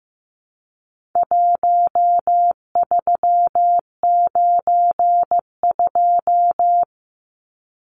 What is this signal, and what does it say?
Morse code, heard as a keyed tone, about 15 wpm, 700 Hz. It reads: 1392
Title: Numbers - Contest Style - 4-Digit 15wpm